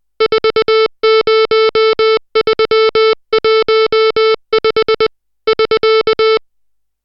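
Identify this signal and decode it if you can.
Morse code, heard as a keyed tone, about 20 wpm, 440 Hz, with noise. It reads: 40315 <SK>
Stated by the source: Morse code drill of digits at 60 characters per minute.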